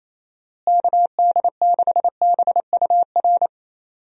Morse code, heard as a keyed tone, about 28 wpm, 700 Hz. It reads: KD6BUR
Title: USA Call Signs - 6 Characters - Repeated 2x - 28wpm